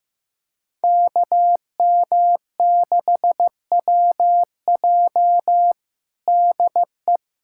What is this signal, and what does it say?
Morse code, heard as a keyed tone, about 15 wpm, 700 Hz. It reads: KM6WJ DE
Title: USA Callsign DE Callsign 15wpm